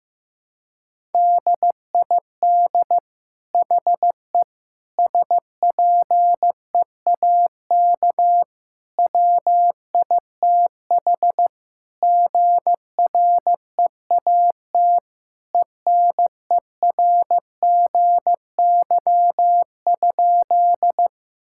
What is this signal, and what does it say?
Morse code, heard as a keyed tone, about 15 wpm, 700 Hz. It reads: DID HE SPEAK WITH GREAT ENERGY?
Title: Sentences from Top 1000 Words - Review Part 6 - 15wpm